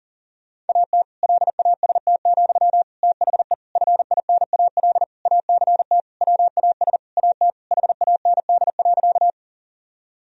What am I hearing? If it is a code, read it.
Morse code, heard as a keyed tone, about 40 wpm, 700 Hz. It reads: AT LAST, THE FINAL ACT WAS AT HAND.
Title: Sentences from Top 1000 Words - Review Part 2 - 40wpm